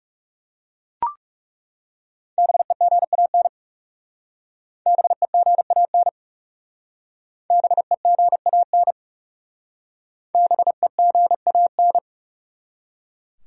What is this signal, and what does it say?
Morse code, heard as a keyed tone, about 35 wpm, 700 Hz. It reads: BEGAN BEGAN BEGAN BEGAN
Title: Speed-Racing Top 300 Words 30wpm